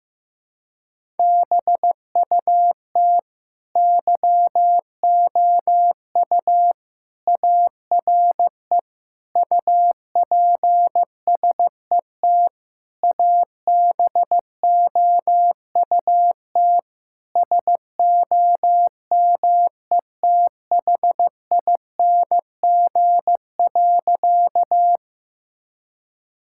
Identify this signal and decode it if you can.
Morse code, heard as a keyed tone, about 15 wpm, 700 Hz. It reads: BUT YOU ARE UPSET ABOUT SOMETHING.
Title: Sentences from Top 1000 Words - Review Part 5 - 15wpm